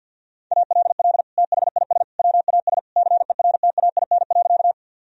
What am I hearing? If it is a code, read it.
Morse code, heard as a keyed tone, about 46 wpm, 700 Hz. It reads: ALL THIS WAS CERTAIN.
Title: Sentences from Top 1000 Words - Review Part 2 - 50wpm